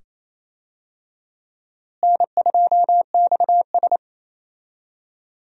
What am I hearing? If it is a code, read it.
Morse code, heard as a keyed tone, about 28 wpm, 700 Hz. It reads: N2XS
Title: USA Call Signs - 4 Characters - Repeated 2x - 28wpm